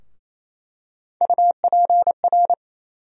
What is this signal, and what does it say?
Morse code, heard as a keyed tone, about 28 wpm, 700 Hz. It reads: UPR
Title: USA Call Sign Postfixes - 3 Letters 28wpm